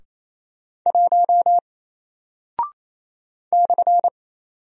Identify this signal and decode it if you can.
Morse code, heard as a keyed tone, about 28 wpm, 700 Hz. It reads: 1 /
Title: Single Character (Letter:Number:Punctuation:Prosign) Alt 28wpm